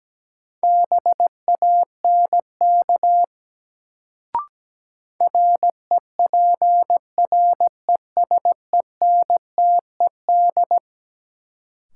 Morse code, heard as a keyed tone, about 17 wpm, 700 Hz. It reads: BANK REPRESENTED
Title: Sets of 5 Words - New Word Drill - 17wpm